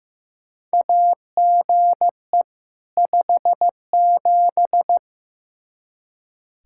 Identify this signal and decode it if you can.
Morse code, heard as a keyed tone, about 15 wpm, 700 Hz. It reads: AGE 57
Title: Practice QSO - Exchange 3 - More Info - 15wpm